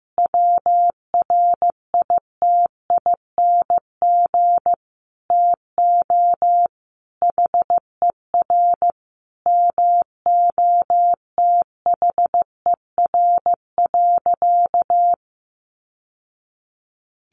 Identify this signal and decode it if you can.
Morse code, heard as a keyed tone, about 15 wpm, 700 Hz. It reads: WRITING TO HER MOTHER.
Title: Sentences from Top 800 Words 15wpm